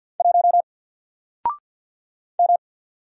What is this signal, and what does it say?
Morse code, heard as a keyed tone, about 48 wpm, 700 Hz. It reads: 1 M
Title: Single Letter-Number Alt 50wpm